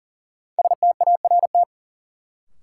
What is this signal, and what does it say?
Morse code, heard as a keyed tone, about 40 wpm, 700 Hz. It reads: START
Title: Sentences from Top 300 Words - New Word Drill - 40wpm